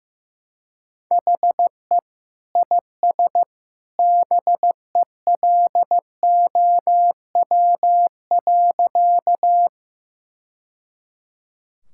Morse code, heard as a keyed tone, about 15 wpm, 700 Hz. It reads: HE IS BELOW.
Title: Sentences from Top 1000 Words - Review Part 5 - 15wpm